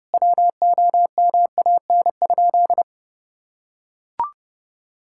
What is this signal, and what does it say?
Morse code, heard as a keyed tone, about 30 wpm, 700 Hz. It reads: WOMAN?
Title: Sentences from Top 1600 Words 30wpm